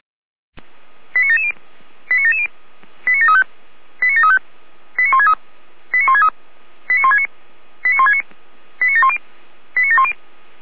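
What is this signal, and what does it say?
Five-tone selective call (ZVEI), heard as quick runs of five tones, tones of about 80 ms. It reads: 89770 89770 89735 89735 89163 89163 89169 89169 89710 89710
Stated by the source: ZVEI five-tone call 89750 received by SDR.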